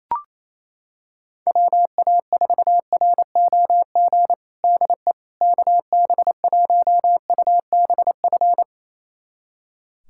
Morse code, heard as a keyed tone, about 27 wpm, 700 Hz. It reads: WA4ROG DE KB1UBF
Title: USA Callsign DE Callsign - 6 Characters - 28wpm